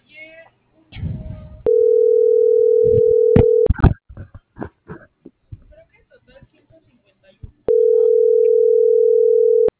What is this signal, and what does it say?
Ringback tone (North America), 440 + 480 Hz.